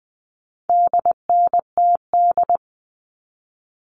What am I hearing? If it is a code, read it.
Morse code, heard as a keyed tone, about 20 wpm, 700 Hz. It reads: DNTD
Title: All Cut Numbers - 4-Digit 20wpm